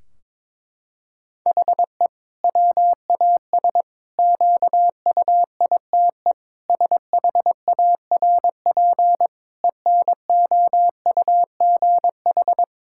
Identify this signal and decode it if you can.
Morse code, heard as a keyed tone, about 22 wpm, 700 Hz. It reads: HE WAS QUITE SHARP ENOUGH
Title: Sentences from Top 1600 Words 22wpm